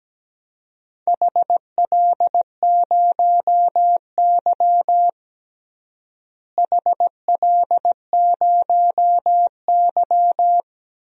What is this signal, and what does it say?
Morse code, heard as a keyed tone, about 17 wpm, 700 Hz. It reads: HL0Y HL0Y
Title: DXCC-100 Call Signs - 4 Characters - Repeated 2x - 17wpm